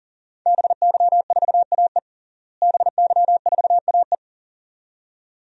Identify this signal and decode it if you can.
Morse code, heard as a keyed tone, about 40 wpm, 700 Hz. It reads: BY4AE BY4AE
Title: DXCC-100 Call Signs - 5 Characters - Repeated 2x - 40wpm